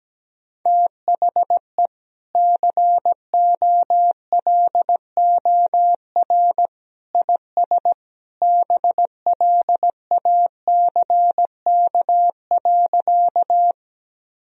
Morse code, heard as a keyed tone, about 17 wpm, 700 Hz. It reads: THE COLOR IS BLACK.
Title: Sentences from Top 1000 Words - Review Part 2 - 17wpm